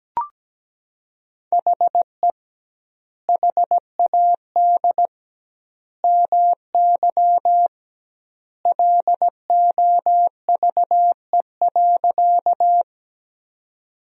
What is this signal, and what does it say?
Morse code, heard as a keyed tone, about 17 wpm, 700 Hz. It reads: HE HAD MY LOVE.
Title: 2x Word-Spacing - Sentences from Top 400 Words - 17wpm